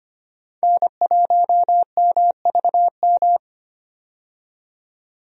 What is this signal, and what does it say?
Morse code, heard as a keyed tone, about 25 wpm, 700 Hz. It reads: N1MVM
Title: USA Call Signs - 5 Characters - 25wpm